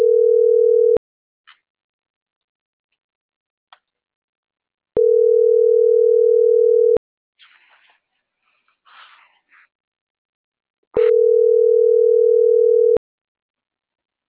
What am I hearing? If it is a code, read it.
Ringback tone (North America), 440 + 480 Hz.